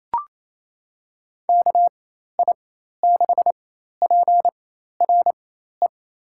Morse code, heard as a keyed tone, about 28 wpm, spaced as for 13 wpm, 700 Hz. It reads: KI6PRE
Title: USA Call Signs - 4x Character Spacing - 28wpm